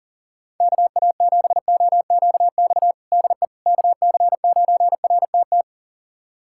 Morse code, heard as a keyed tone, about 40 wpm, 700 Hz. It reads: KA7OQX DE KC9RTT
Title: USA Callsign DE Callsign - 6 Characters - 40wpm